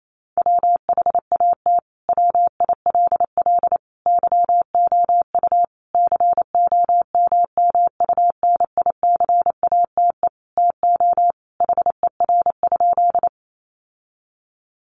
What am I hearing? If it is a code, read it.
Morse code, heard as a keyed tone, about 28 wpm, 700 Hz. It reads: WHAT WILL YOU COMMUNICATE TO HER?
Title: Sentences from Top 1300 Words 28wpm